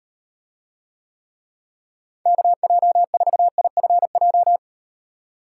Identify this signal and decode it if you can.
Morse code, heard as a keyed tone, about 38 wpm, 700 Hz. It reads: KJ4IFJ